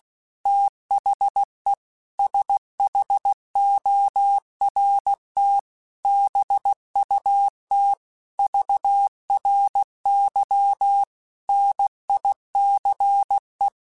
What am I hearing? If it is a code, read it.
Morse code, heard as a keyed tone, about 16 wpm, 800 Hz. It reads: THE SHORT BUT VRY NICE
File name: QSOs / QSOs-mp3s / WA7QOX 2@15